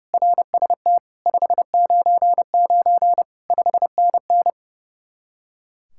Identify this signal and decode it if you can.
Morse code, heard as a keyed tone, about 35 wpm, 700 Hz. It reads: RST 599 5NN
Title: Speed-Racing Common QSO Phrases - 30wpm